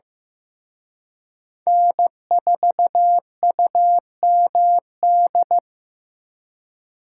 Morse code, heard as a keyed tone, about 18 wpm, 700 Hz. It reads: N4UMD
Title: Speed Racing - USA Call Signs 15wpm